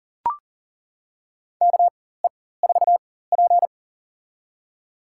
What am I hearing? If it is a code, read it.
Morse code, heard as a keyed tone, about 40 wpm, spaced as for 19 wpm, 700 Hz. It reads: KE4P
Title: USA Call Signs - 4 Characters - 4x Character Spacing - 40wpm